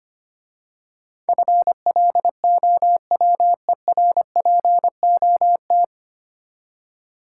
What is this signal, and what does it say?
Morse code, heard as a keyed tone, about 25 wpm, 700 Hz. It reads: FLOWERPOT
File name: Easy Words - Speed-Racing with Hints - 25wpm